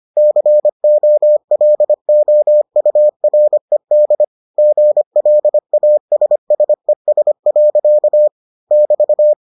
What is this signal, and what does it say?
Morse code, heard as a keyed tone, about 25 wpm, 600 Hz. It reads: COLOURED GLASSES. <BT>